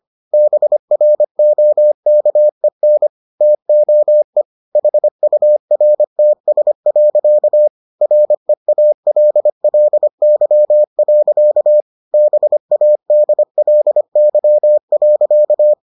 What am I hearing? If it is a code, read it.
Morse code, heard as a keyed tone, about 25 wpm, 600 Hz. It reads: BROKEN TOE HURTS. REALLY. BADLY.